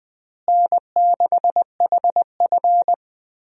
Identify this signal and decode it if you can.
Morse code, heard as a keyed tone, about 20 wpm, 700 Hz. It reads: N6HF
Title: USA Call Signs - 4 Characters - Repeated 2x - 20wpm